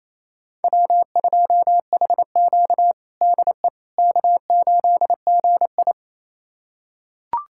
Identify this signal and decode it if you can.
Morse code, heard as a keyed tone, about 28 wpm, 700 Hz. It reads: W2HQ DE K8GI
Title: USA Callsign DE Callsign - 4 Characters - 28wpm